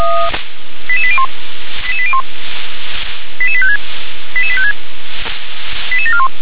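Five-tone selective call (ZVEI), heard as quick runs of five tones, tones of about 70 ms. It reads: 88801 88801 88856 88856 88741
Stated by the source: ZVEI five-tone call 88774 received on SDR.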